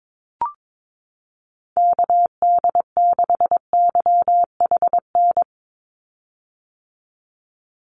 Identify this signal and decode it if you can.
Morse code, heard as a keyed tone, about 22 wpm, 700 Hz. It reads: KD6YHN
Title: USA Call Signs 22wpm